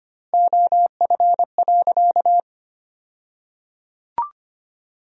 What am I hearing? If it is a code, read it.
Morse code, heard as a keyed tone, about 25 wpm, 700 Hz. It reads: OF.
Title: 2x Word-Spacing - Sentences from Top 300 Words - 25wpm